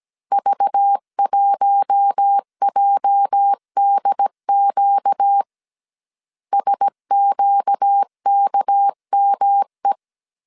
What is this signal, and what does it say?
Morse code, heard as a keyed tone, about 17 wpm, 800 Hz. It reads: V1JDQ SQKME
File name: CW / DET / 1 9 1500C S A 16wpm 2828